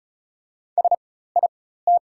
Morse code, heard as a keyed tone, about 36 wpm, spaced as for 16 wpm, 690 Hz. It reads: SIT
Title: 4x Character Spacing - Top 500 Words - 35wpm